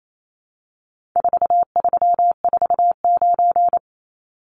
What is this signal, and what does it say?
Morse code, heard as a keyed tone, about 28 wpm, 700 Hz. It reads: 4349
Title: Numbers - 4-Digit 28wpm